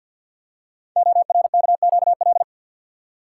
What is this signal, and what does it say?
Morse code, heard as a keyed tone, about 50 wpm, 700 Hz. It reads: ORKQL
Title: Code Groups - 5 Letters - 50wpm